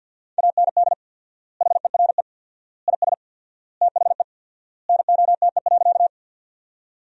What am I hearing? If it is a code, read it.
Morse code, heard as a keyed tone, about 50 wpm, 700 Hz. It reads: AND HERE IS THE NOTE.
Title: 4x Word-Spacing - Sentences from Top 500 Words - 50wpm